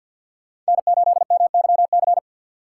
Morse code, heard as a keyed tone, about 65 wpm, spaced as for 57 wpm, 700 Hz. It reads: N8MYC
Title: Speed Racing - USA Call Signs 50wpm